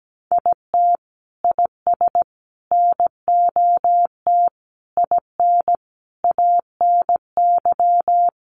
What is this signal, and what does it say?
Morse code, heard as a keyed tone, about 17 wpm, 700 Hz. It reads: IT IS NOT IN ANY
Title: Sentences from Top 1200 Words 17wpm